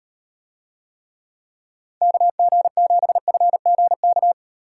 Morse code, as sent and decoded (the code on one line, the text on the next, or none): -.- --. --... ..-. --. -.-
KG7FGK